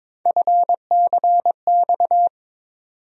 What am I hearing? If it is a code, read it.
Morse code, heard as a keyed tone, about 22 wpm, 700 Hz. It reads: FCX
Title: USA Call Sign Postfixes - 3 Letters 22wpm